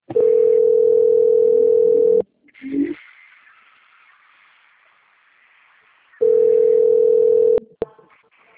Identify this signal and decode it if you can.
Ringback tone (North America), 440 + 480 Hz.
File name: out- - - - - .9738